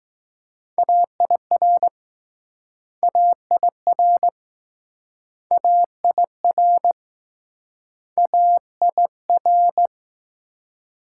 Morse code, heard as a keyed tone, about 20 wpm, 700 Hz. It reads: AIR AIR AIR AIR